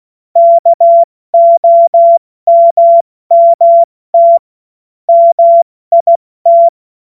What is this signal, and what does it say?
Morse code, heard as a keyed tone, about 16 wpm, 680 Hz. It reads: KOMMT MIT
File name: Morsekurs / Witze / Witz 21. September 2021, 70 F80